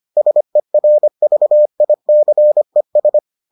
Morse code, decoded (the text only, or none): SERVICES